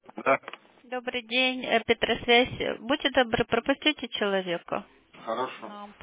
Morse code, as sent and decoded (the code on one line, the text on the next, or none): none